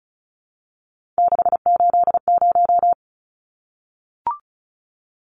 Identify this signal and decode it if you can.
Morse code, heard as a keyed tone, about 34 wpm, 700 Hz. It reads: 680
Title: Numbers - 3-Digit 35wpm